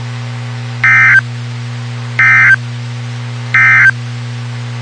Emergency Alert System (SAME) tones, heard as harsh buzzing bursts. It is heard from 0.7 to 4.0 s.
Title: WXK91 NOAA Weather Radio